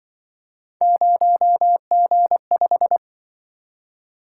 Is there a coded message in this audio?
Yes: Morse code from 0.8 to 3.0 s, heard as a keyed tone, about 26 wpm, 700 Hz. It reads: 0G5